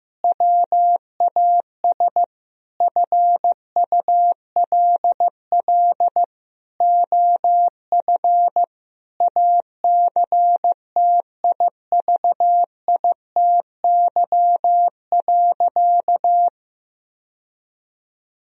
Morse code, heard as a keyed tone, about 15 wpm, 700 Hz. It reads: WAS FULL OF ACTIVITY.
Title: Sentences from Top 1000 Words - Review Part 4 - 15wpm